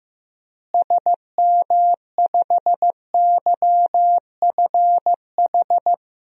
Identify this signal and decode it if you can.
Morse code, heard as a keyed tone, about 15 wpm, 700 Hz. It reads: SM5YFH
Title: DXCC-100 Call Signs - 6 Characters - Repeated 2x - 15wpm